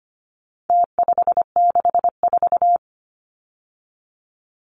Morse code, heard as a keyed tone, about 25 wpm, 700 Hz. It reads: T564
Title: Numbers - Contest Style - 4-Digit 25wpm